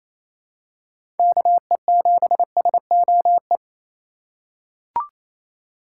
Morse code, heard as a keyed tone, about 28 wpm, 700 Hz. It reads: KE7SOE
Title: USA Call Signs - 2x Character Spacing - 28wpm